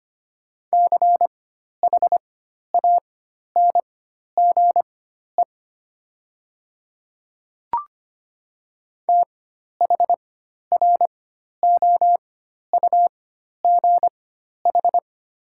Morse code, heard as a keyed tone, about 25 wpm, spaced as for 12 wpm, 700 Hz. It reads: CHANGE THROUGH
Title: 4x Character Spacing - Top 500 Words - 25wpm